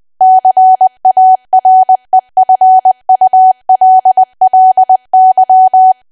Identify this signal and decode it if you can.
Morse code, heard as a keyed tone, about 20 wpm, 750 Hz. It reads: CAREFULLY
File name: Archive / 20 WPM / 20WPM